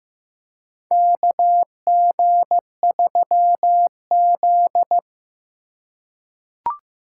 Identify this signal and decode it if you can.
Morse code, heard as a keyed tone, about 15 wpm, 700 Hz. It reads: KG3Z
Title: USA Call Signs - 2x Character Spacing - 15wpm